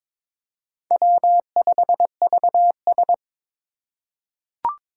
Morse code, heard as a keyed tone, about 22 wpm, 700 Hz. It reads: W5VS
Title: USA Call Signs - 3x Character Spacing - 22wpm